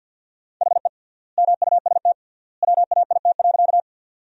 Morse code, heard as a keyed tone, about 53 wpm, 700 Hz. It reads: HE MUST WAIT.
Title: 3x Word-Spacing - Sentences from Top 500 Words - 50wpm